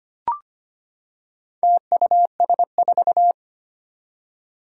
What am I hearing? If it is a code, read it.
Morse code, heard as a keyed tone, about 25 wpm, 700 Hz. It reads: TUS4
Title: Four Challenging Letter-Number Combinations 25wpm